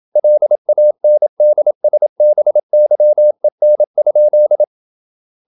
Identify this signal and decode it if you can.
Morse code, heard as a keyed tone, about 27 wpm, 600 Hz. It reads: LANDSBYEN?
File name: Ferie QRQ 27wpm